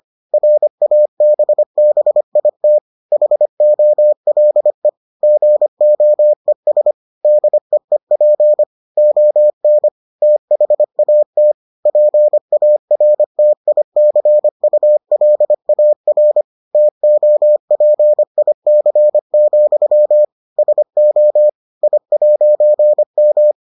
Morse code, heard as a keyed tone, about 25 wpm, 600 Hz. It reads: RABBIT HOLE GOES DEEP ON THAT PARTICULAR TOPIC, SO I'M